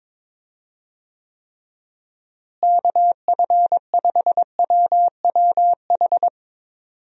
Morse code, heard as a keyed tone, about 22 wpm, 700 Hz. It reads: KF5WWH
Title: USA Call Signs - 6 Characters - Repeated 2x - 22wpm